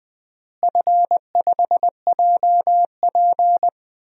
Morse code, heard as a keyed tone, about 20 wpm, 700 Hz. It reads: F5JP
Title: DXCC-100 Call Signs - 4 Characters - Repeated 2x - 20wpm